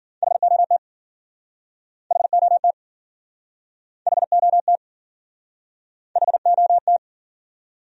Morse code, heard as a keyed tone, about 52 wpm, 700 Hz. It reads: HOT HOT HOT HOT